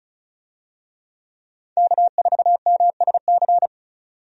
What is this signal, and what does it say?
Morse code, heard as a keyed tone, about 36 wpm, 700 Hz. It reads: K4MSC
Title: USA Call Signs - 5 Characters - 35wpm